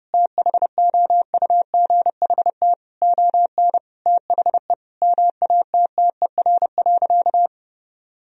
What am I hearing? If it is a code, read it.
Morse code, heard as a keyed tone, about 32 wpm, 700 Hz. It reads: THOUGHT ON THE MATTER.
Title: Sentences from Top 1000 Words - Review Part 1 - 30wpm